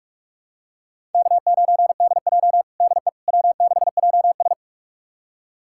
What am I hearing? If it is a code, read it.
Morse code, heard as a keyed tone, about 46 wpm, 700 Hz. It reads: K9DJ DE W6JS
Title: USA Callsign DE Callsign - 4 Characters - 45wpm